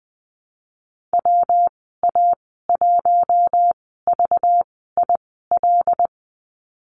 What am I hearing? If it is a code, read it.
Morse code, heard as a keyed tone, about 20 wpm, spaced as for 14 wpm, 700 Hz. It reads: WA1VIL
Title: 193 Lesson 31g - Letter v - Call Signs 20wpm